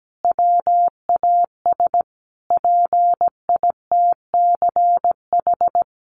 Morse code, heard as a keyed tone, about 17 wpm, 700 Hz. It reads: WAS PITCH
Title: Sentences from Top 1300 Words 17wpm